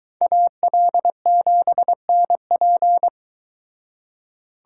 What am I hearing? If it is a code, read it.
Morse code, heard as a keyed tone, about 23 wpm, 700 Hz. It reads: AL7NP